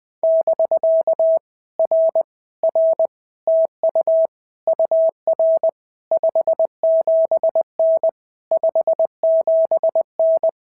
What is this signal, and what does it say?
Morse code, heard as a keyed tone, about 20 wpm, 650 Hz. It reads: <BK> R R TU UR 57N 57N
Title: POTA QSO Practice - 20wpm